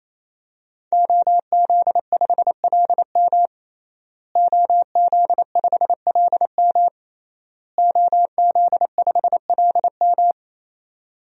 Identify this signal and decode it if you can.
Morse code, heard as a keyed tone, about 28 wpm, 700 Hz. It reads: OZ5LM OZ5LM OZ5LM
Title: DXCC-100 Call Signs - 5 Characters - Repeated 3x - 28wpm